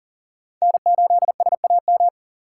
Morse code, heard as a keyed tone, about 40 wpm, 700 Hz. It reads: N8SAM
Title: USA Call Signs - 5 Characters - 2x Character Spacing - 40wpm